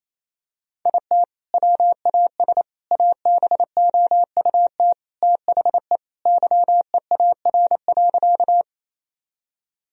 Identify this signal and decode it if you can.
Morse code, heard as a keyed tone, about 28 wpm, 700 Hz. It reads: IT WAS ABOUT THE YEAR.
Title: Sentences from Top 1000 Words - Review Part 1 - 28wpm